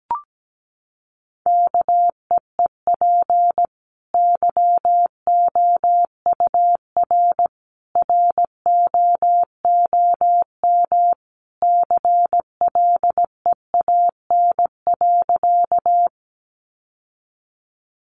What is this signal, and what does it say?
Morse code, heard as a keyed tone, about 17 wpm, 700 Hz. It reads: KEEP YOUR ROOM CLEAN.
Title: Sentences from Top 1100 Words 17wpm